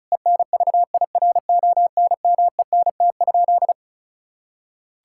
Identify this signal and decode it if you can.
Morse code, heard as a keyed tone, about 36 wpm, 700 Hz. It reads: ENVIRONMENT?
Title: Sentences from Top 1000 Words - Review Part 5 - 35wpm